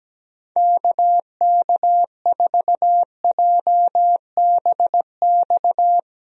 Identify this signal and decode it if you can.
Morse code, heard as a keyed tone, about 17 wpm, 700 Hz. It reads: KK4JBX
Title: USA Call Signs - 6 Characters - Repeated 2x - 17wpm